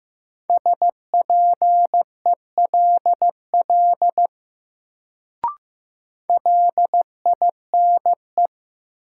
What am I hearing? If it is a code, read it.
Morse code, heard as a keyed tone, about 15 wpm, 700 Hz. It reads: SPELL LINE
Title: Sentences from Top 1000 Words - Review Part 1 - New Word Drill - 15wpm